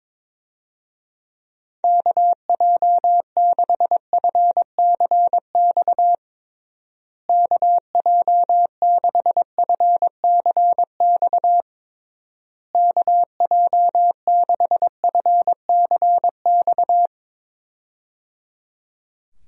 Morse code, heard as a keyed tone, about 22 wpm, 700 Hz. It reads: KJ6FCX KJ6FCX KJ6FCX
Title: USA Call Signs - 6 Characters - Repeated 3x - 22wpm